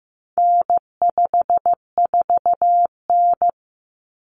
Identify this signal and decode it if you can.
Morse code, heard as a keyed tone, about 15 wpm, 700 Hz. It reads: N54N
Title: Common Cut Numbers - 4-Digit 15wpm